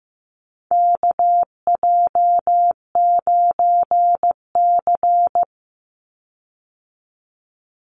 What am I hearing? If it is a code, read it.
Morse code, heard as a keyed tone, about 15 wpm, 700 Hz. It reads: KJ9C
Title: Call Signs Repeated 3x - 15wpm